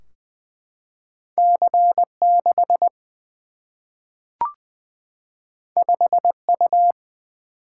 Morse code, heard as a keyed tone, about 20 wpm, 700 Hz. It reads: C6 5U
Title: Single-Letter Single-Number 20wpm